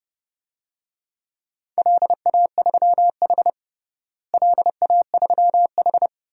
Morse code, heard as a keyed tone, about 30 wpm, 700 Hz. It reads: LA3H LA3H
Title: DXCC-100 Call Signs - 4 Characters - Repeated 3x - 30wpm